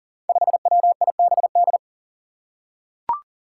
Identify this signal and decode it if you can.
Morse code, heard as a keyed tone, about 40 wpm, 700 Hz. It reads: 5WIBD
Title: Five Challenging Letter-Number Combinations 40wpm